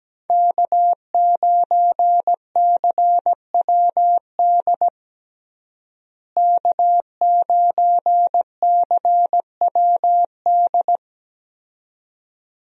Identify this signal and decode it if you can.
Morse code, heard as a keyed tone, about 17 wpm, 700 Hz. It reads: K9CWD K9CWD
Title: USA Call Signs - 5 Characters - Repeated 2x - 17wpm